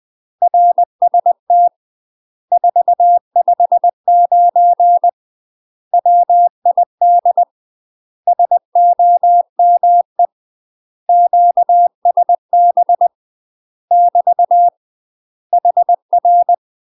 Morse code, heard as a keyed tone, about 20 wpm, 700 Hz. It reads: RST 459 WID SOME QSB <BT> HR
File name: UK MASTER QSOs TRAINING/20WPM